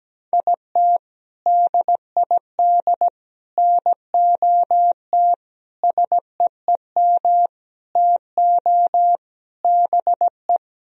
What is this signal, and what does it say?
Morse code, heard as a keyed tone, about 17 wpm, 700 Hz. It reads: IT DID NOT SEEM TO BE
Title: Sentences from Top 1000 Words - Review Part 1 - 17wpm